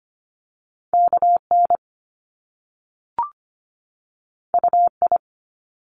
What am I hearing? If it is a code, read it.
Morse code, heard as a keyed tone, about 25 wpm, 700 Hz. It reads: KN UI
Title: Top Two Letter Combinations in Words 25wpm